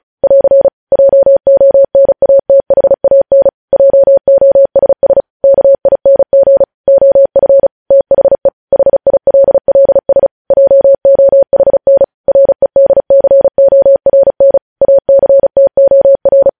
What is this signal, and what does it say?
Morse code, heard as a keyed tone, about 36 wpm, 550 Hz. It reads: <AR> JONATHAN JOSS KING OF THE HILLS JOHN REDCORN ACTOR
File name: podcast / News headlines-1x-35x35